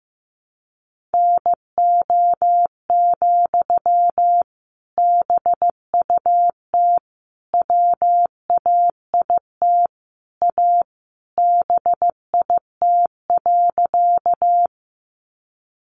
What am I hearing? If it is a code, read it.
Morse code, heard as a keyed tone, about 15 wpm, 700 Hz. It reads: NO, BUT WAIT A BIT.
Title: Sentences from Top 900 Words 15wpm